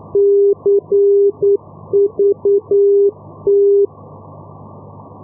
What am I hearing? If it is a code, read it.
Morse code, heard as a keyed tone, about 9 wpm, 400 Hz. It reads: CVT
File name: europe / e-448 ndb madrid-cuatro vientos cvt-285 np madrid 2016-05